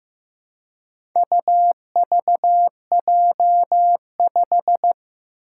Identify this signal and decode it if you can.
Morse code, heard as a keyed tone, about 15 wpm, 700 Hz. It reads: UVJ5
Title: Four Challenging Letter-Number Combinations 15wpm